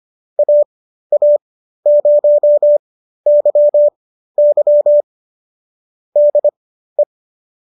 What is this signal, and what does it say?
Morse code, heard as a keyed tone, about 25 wpm, spaced as for 13 wpm, 590 Hz. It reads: AA0YY DE
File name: qso213 13